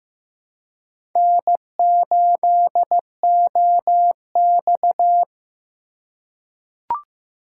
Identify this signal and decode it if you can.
Morse code, heard as a keyed tone, about 15 wpm, 700 Hz. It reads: N8OX
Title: USA Call Signs - 4 Characters - 3x Character Spacing - 15wpm